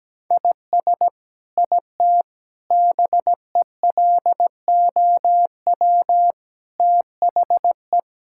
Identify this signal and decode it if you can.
Morse code, heard as a keyed tone, about 17 wpm, 700 Hz. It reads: IS IT BELOW THE
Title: Sentences from Top 1000 Words - Review Part 5 - 17wpm